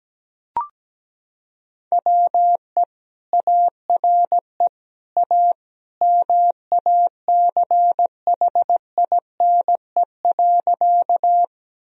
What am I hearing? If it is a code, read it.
Morse code, heard as a keyed tone, about 17 wpm, 700 Hz. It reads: WE ARE A MACHINE.
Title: Sentences from Top 1000 Words - Review Part 2 - 17wpm